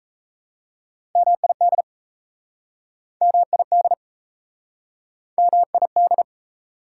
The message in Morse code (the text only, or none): MID MID MID